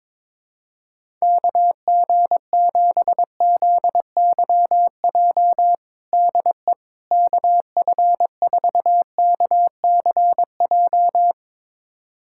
Morse code, heard as a keyed tone, about 22 wpm, 700 Hz. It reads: KG7ZYJ DE KF4KCJ
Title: USA Callsign DE Callsign - 6 Characters - 22wpm